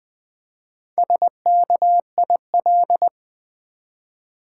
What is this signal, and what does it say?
Morse code, heard as a keyed tone, about 20 wpm, 700 Hz. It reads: SKIL
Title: Build A Word - 1 Letter At A Time - 3 to 5-Letter Words - 20wpm